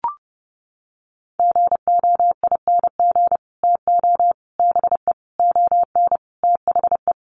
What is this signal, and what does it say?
Morse code, heard as a keyed tone, about 30 wpm, 700 Hz. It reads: GOING TO BE ON THE
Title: Sets of 5 Words Encore2 30wpm